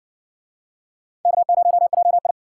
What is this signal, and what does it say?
Morse code, heard as a keyed tone, about 57 wpm, 700 Hz. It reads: K0JI